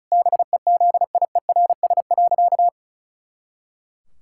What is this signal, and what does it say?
Morse code, heard as a keyed tone, about 36 wpm, 700 Hz. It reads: BEZIERS.
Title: Pangram Challenge v2 - 35wpm